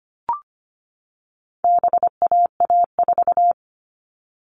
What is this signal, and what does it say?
Morse code, heard as a keyed tone, about 25 wpm, 700 Hz. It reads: BAA4
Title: All Cut Numbers - 4-Digit 25wpm